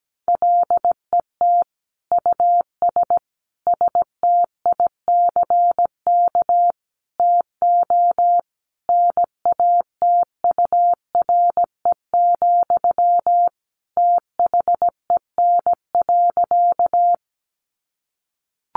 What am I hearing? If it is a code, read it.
Morse code, heard as a keyed tone, about 17 wpm, 700 Hz. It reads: LET US STICK TO NATURE, THEN.
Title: Sentences from Top 1200 Words 17wpm